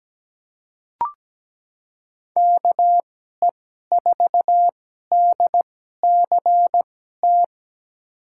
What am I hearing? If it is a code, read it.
Morse code, heard as a keyed tone, about 17 wpm, spaced as for 12 wpm, 700 Hz. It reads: KE4DCT
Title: USA Call Signs - 6 Characters - 2x Character Spacing - 17wpm